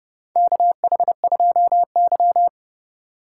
Morse code, heard as a keyed tone, about 30 wpm, 700 Hz. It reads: KH2Y
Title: USA Call Signs - 4 Characters - 2x Character Spacing - 30wpm